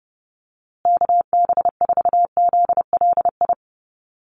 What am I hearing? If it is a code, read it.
Morse code, heard as a keyed tone, about 30 wpm, 700 Hz. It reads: KB4ZLI
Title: USA Call Signs Repeated 3x 30wpm